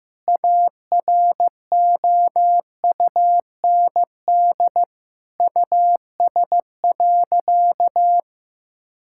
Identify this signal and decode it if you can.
Morse code, heard as a keyed tone, about 15 wpm, 700 Hz. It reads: AROUND US.
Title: Sentences from Top 1000 Words - Review Part 5 - 15wpm